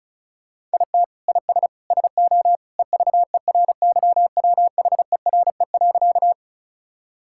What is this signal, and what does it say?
Morse code, heard as a keyed tone, about 34 wpm, 700 Hz. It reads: IT IS SO EVERYWHERE.
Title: Sentences from Top 1600 Words 35wpm